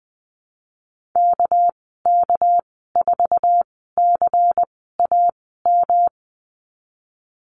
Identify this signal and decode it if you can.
Morse code, heard as a keyed tone, about 20 wpm, spaced as for 15 wpm, 700 Hz. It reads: KK4CAM
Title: 231 Lesson 37g - Letter x - Call Signs 20wpm